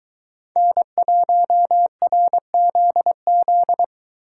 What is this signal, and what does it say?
Morse code, heard as a keyed tone, about 23 wpm, 700 Hz. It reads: N1RZZ